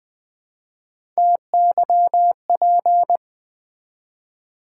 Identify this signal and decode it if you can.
Morse code, heard as a keyed tone, about 20 wpm, 700 Hz. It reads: TYP
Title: Build A Word - 1 Letter At A Time - 3 to 5-Letter Words - 20wpm